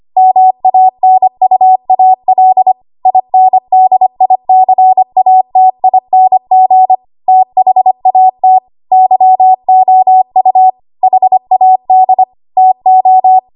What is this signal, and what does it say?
Morse code, heard as a keyed tone, about 25 wpm, 750 Hz. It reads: MANUAL INDICATING THAT YOU HAD TO
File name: Archive / 25 WPM / 25WPM